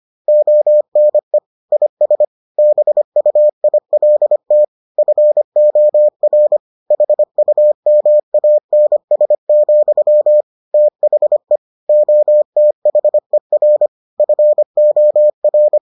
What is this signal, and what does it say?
Morse code, heard as a keyed tone, about 25 wpm, 600 Hz. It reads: ONE IS BUILT FOR HUMANS, THE OTHER FOR